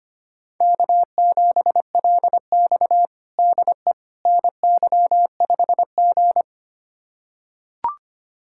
Morse code, heard as a keyed tone, about 25 wpm, 700 Hz. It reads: K7LX DE NY5G
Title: USA Callsign DE Callsign - 4 Characters - 25wpm